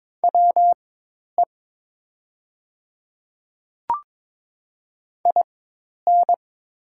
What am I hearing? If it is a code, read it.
Morse code, heard as a keyed tone, about 22 wpm, spaced as for 10 wpm, 700 Hz. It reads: WE IN
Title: 4x Character Spacing - Top 100 Words - 22wpm